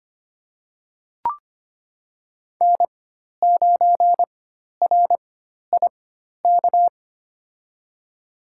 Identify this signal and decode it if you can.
Morse code, heard as a keyed tone, about 25 wpm, spaced as for 12 wpm, 700 Hz. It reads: N9RIK
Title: USA Call Signs - 5 Characters - 4x Character Spacing - 25wpm